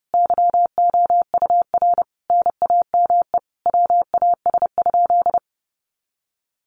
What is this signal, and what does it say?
Morse code, heard as a keyed tone, about 28 wpm, 700 Hz. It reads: YOUR NAME WAS?
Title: Sentences from Top 200 Words 30wpm